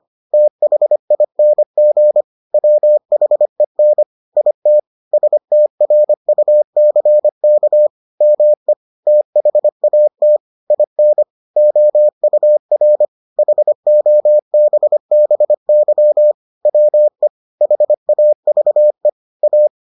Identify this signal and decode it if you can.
Morse code, heard as a keyed tone, about 25 wpm, 600 Hz. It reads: THING WHEN IT STRUCK ME THAT IN OUR HOBBY WE HAVE A